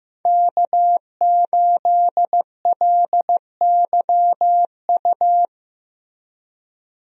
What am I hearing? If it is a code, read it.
Morse code, heard as a keyed tone, about 15 wpm, 700 Hz. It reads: K8LYU
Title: USA Call Signs - 5 Characters - Repeated 3x - 15wpm